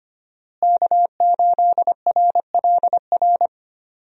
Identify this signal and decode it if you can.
Morse code, heard as a keyed tone, about 25 wpm, 700 Hz. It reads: K8RLR
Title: USA Call Signs - 5 Characters - Repeated 3x - 25wpm